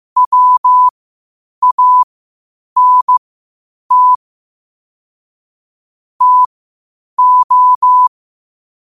Morse code, heard as a keyed tone, about 15 wpm, spaced as for 8 wpm, 1000 Hz. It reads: WANT TO
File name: qotdfiles / 8wpm